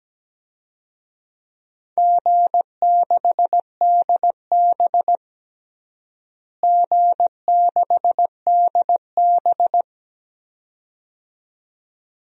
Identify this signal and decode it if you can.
Morse code, heard as a keyed tone, about 17 wpm, 700 Hz. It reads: G6DB G6DB
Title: DXCC-100 Call Signs - 4 Characters - Repeated 2x - 17wpm